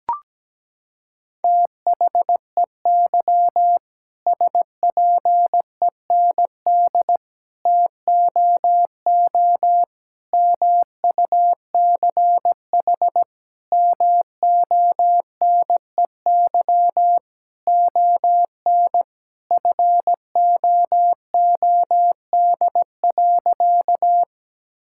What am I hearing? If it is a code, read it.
Morse code, heard as a keyed tone, about 17 wpm, 700 Hz. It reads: THEY SPEND TOO MUCH MONEY ON FOOD.
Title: Sentences from Top 1000 Words - Review Part 5 - 17wpm